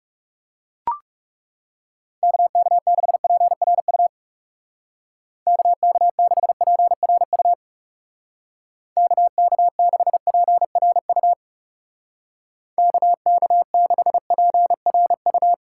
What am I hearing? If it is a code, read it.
Morse code, heard as a keyed tone, about 38 wpm, 700 Hz. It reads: KK6PRU KK6PRU KK6PRU KK6PRU